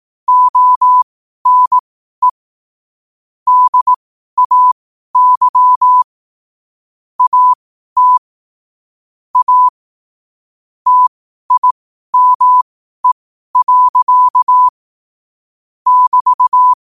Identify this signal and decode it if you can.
Morse code, heard as a keyed tone, about 18 wpm, spaced as for 12 wpm, 1000 Hz. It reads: ONE DAY AT A TIME. <BT>
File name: qotdfiles / 12wpm